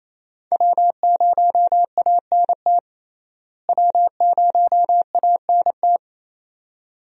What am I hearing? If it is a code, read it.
Morse code, heard as a keyed tone, about 27 wpm, 700 Hz. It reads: W0ANT W0ANT
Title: USA Call Signs - 5 Characters - Repeated 2x - 28wpm